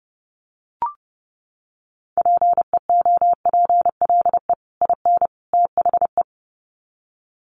Morse code, heard as a keyed tone, about 30 wpm, 700 Hz. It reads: PEOPLE IN THE
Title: Sets of 3 Words 30wpm